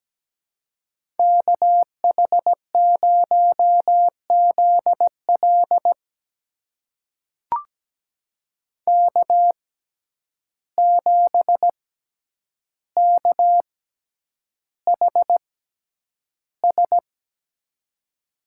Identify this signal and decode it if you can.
Morse code, heard as a keyed tone, about 17 wpm, spaced as for 9 wpm, 700 Hz. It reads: KH0ZL K7KHS
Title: USA Call Signs - 5 Characters - 6x Character Spacing - 17wpm